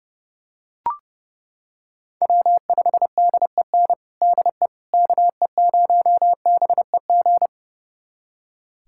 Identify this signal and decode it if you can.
Morse code, heard as a keyed tone, about 30 wpm, 700 Hz. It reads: W5DEN DE KE0BEG
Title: USA Callsign DE Callsign 30wpm